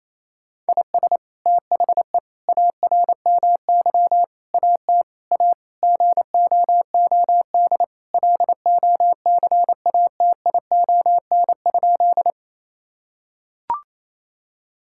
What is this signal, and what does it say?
Morse code, heard as a keyed tone, about 28 wpm, 700 Hz. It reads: IS THE ARMY AT A GOOD LOCATION?
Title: Sentences from Top 1000 Words - Review Part 6 - 28wpm